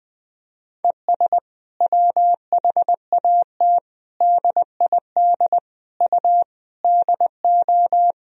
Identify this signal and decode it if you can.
Morse code, heard as a keyed tone, about 20 wpm, 700 Hz. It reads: ES WHAT DID U DO
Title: Sentences from Top 100 Words and Abbreviations 20wpm